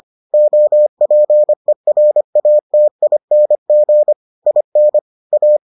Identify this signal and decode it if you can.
Morse code, heard as a keyed tone, about 25 wpm, 600 Hz. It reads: OPERATING IN A